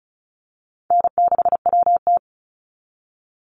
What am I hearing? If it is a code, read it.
Morse code, heard as a keyed tone, about 34 wpm, 700 Hz. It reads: N6WT
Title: NA SOTA Chaser Callsigns 35wpm